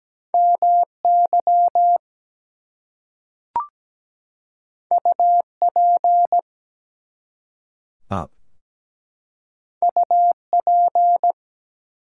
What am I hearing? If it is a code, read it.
Morse code, heard as a keyed tone, about 17 wpm, 700 Hz. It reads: MY UP UP
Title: Two Letter Words - 17wpm